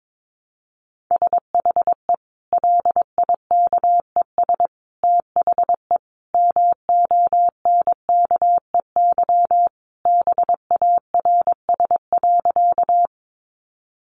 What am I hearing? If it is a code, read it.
Morse code, heard as a keyed tone, about 22 wpm, 700 Hz. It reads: SHE LIKES THE MONKEY BARS.
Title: Easy Phrases 22wpm